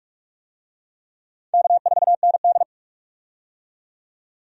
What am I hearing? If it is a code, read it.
Morse code, heard as a keyed tone, about 42 wpm, 700 Hz. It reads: K4ND
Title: USA Call Signs - 4 Characters - 45wpm